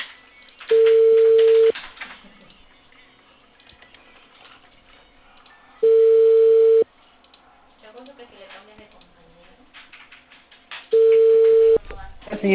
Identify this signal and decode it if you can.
Ringback tone (China, 450 Hz), with speech.